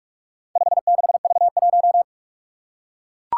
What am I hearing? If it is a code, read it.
Morse code, heard as a keyed tone, about 44 wpm, 700 Hz. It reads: 56V1
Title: Four Challenging Letter-Number Combinations 45wpm